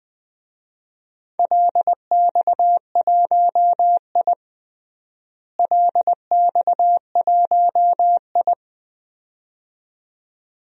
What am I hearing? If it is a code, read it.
Morse code, heard as a keyed tone, about 20 wpm, 700 Hz. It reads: LX1I LX1I
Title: DXCC-100 Call Signs - 4 Characters - Repeated 2x - 20wpm